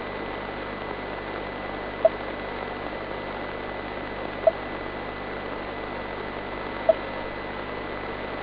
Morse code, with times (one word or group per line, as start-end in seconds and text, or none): none